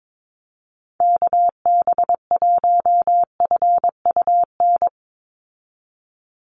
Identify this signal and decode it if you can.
Morse code, heard as a keyed tone, about 22 wpm, 700 Hz. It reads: KB1FUN
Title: USA Call Signs 22wpm